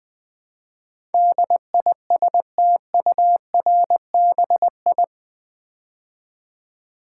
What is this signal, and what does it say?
Morse code, heard as a keyed tone, about 21 wpm, 700 Hz. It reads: DISTURBI
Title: Build A Word - 4 Letters At A Time - 10-Letter Words - 20wpm